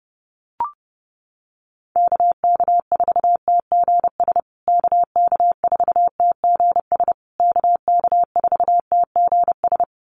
Morse code, heard as a keyed tone, about 30 wpm, 700 Hz. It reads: KK4TGS KK4TGS KK4TGS
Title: USA Call Signs Repeated 3x 30wpm